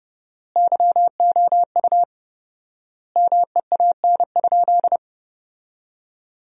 Morse code, heard as a keyed tone, about 30 wpm, 700 Hz. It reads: YOU MEAN?
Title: 4x Word-Spacing - Sentences from Top 500 Words - 30wpm